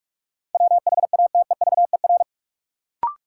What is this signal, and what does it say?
Morse code, heard as a keyed tone, about 44 wpm, 700 Hz. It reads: WHATEVER
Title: Sentences from Top 1100 Words - New Word Drill - 45wpm